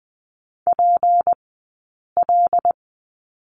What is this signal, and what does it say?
Morse code, heard as a keyed tone, about 20 wpm, spaced as for 8 wpm, 700 Hz. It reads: PL
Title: 151 Lesson 25d - Letter p - Two Characters 20wpm